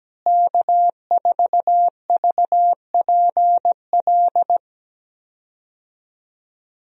Morse code, heard as a keyed tone, about 17 wpm, 700 Hz. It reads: K4VPL
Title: USA Call Signs - 5 Characters - Repeated 3x - 17wpm